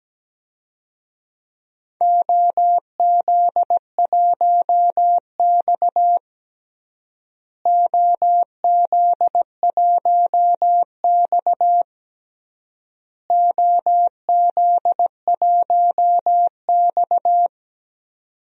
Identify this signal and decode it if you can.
Morse code, heard as a keyed tone, about 17 wpm, 700 Hz. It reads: OZ1X OZ1X OZ1X
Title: DXCC-100 Call Signs - 4 Characters - Repeated 3x - 17wpm